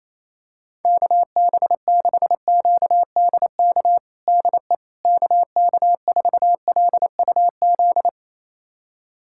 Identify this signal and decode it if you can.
Morse code, heard as a keyed tone, about 28 wpm, 700 Hz. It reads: KB6QDK DE KK4LUZ
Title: USA Callsign DE Callsign 28wpm